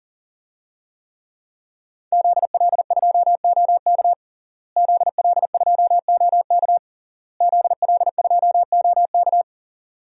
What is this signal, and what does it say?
Morse code, heard as a keyed tone, about 40 wpm, 700 Hz. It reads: ZL2OK ZL2OK ZL2OK
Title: DXCC-100 Call Signs - 5 Characters - Repeated 3x - 40wpm